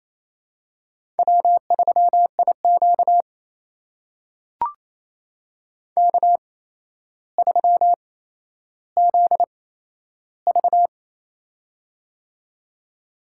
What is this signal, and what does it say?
Morse code, heard as a keyed tone, about 28 wpm, spaced as for 12 wpm, 700 Hz. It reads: W3IQ K3ZV
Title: USA Call Signs - 4 Characters - 8x Character Spacing - 28wpm